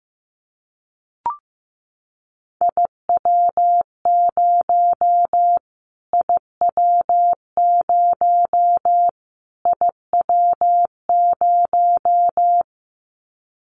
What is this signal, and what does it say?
Morse code, heard as a keyed tone, about 15 wpm, 700 Hz. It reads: IW0 IW0 IW0
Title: Callsign Prefixes for Top 25 DXCC - Repeated 3x - 15wpm